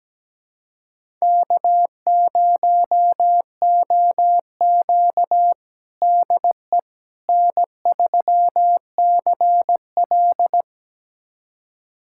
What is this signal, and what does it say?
Morse code, heard as a keyed tone, about 17 wpm, 700 Hz. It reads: K0OQ DE N3CL
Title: USA Callsign DE Callsign - 4 Characters - 17wpm